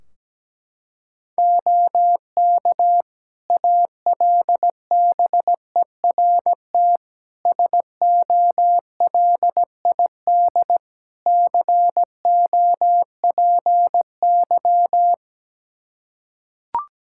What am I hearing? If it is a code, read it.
Morse code, heard as a keyed tone, about 17 wpm, 700 Hz. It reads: OK ALBERT SOLID COPY
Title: Practice QSO - Exchange 3 - More Info - 17wpm